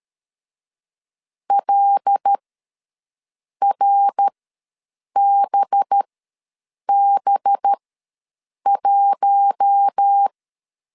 Morse code, heard as a keyed tone, about 13 wpm, spaced as for 7 wpm, 800 Hz. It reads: LRBB1
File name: 5 LB ws 12wpm